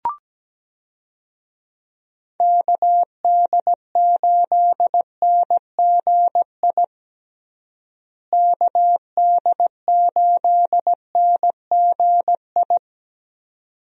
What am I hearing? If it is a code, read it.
Morse code, heard as a keyed tone, about 17 wpm, 700 Hz. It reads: KD8NGI KD8NGI
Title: USA Call Signs - 6 Characters - Repeated 2x - 17wpm